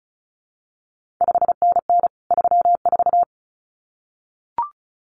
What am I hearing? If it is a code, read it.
Morse code, heard as a keyed tone, about 36 wpm, 700 Hz. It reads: 5NN 34
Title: CQ WW DX Contest Exchange Encore - 35wpm